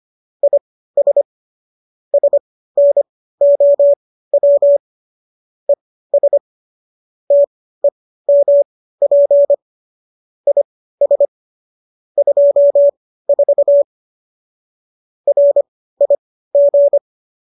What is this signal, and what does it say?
Morse code, heard as a keyed tone, about 25 wpm, spaced as for 15 wpm, 570 Hz. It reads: IS SNOW ES TEMP IS 24 RIG